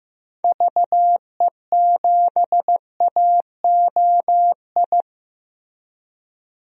Morse code, heard as a keyed tone, about 15 wpm, 700 Hz. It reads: VE7AOI
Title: DXCC-100 Call Signs - 6 Characters - Repeated 3x - 15wpm